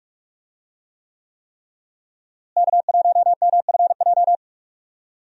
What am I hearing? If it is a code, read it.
Morse code, heard as a keyed tone, about 46 wpm, 700 Hz. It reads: K1MFJ
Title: USA Call Signs - 5 Characters - Repeated 3x - 45wpm